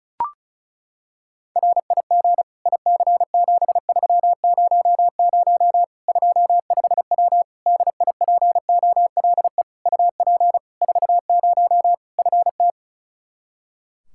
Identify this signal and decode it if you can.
Morse code, heard as a keyed tone, about 36 wpm, 700 Hz. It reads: RIG IC7300 25W DIPOLE UP 40 FT
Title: K1USN SST QSO Practice 35wpm